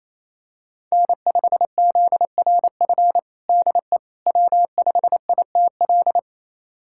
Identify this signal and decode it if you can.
Morse code, heard as a keyed tone, about 28 wpm, 700 Hz. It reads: N5ZRF DE W5ITL
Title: USA Callsign DE Callsign - 5 Characters - 28wpm